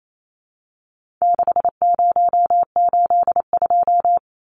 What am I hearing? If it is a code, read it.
Morse code, heard as a keyed tone, about 28 wpm, 700 Hz. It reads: 6082
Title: Numbers - 4-Digit 28wpm